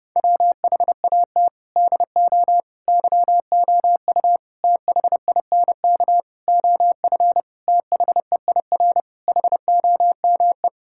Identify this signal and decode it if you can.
Morse code, heard as a keyed tone, about 30 wpm, 700 Hz. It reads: WHAT DO YOU THINK OF THEIR HOME
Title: Sentences from Top 1000 Words - Review Part 5 - 30wpm